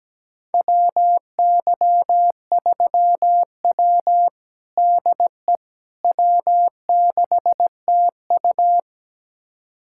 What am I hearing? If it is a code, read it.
Morse code, heard as a keyed tone, about 17 wpm, 700 Hz. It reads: WY3W DE W6TU
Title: USA Callsign DE Callsign - 4 Characters - 17wpm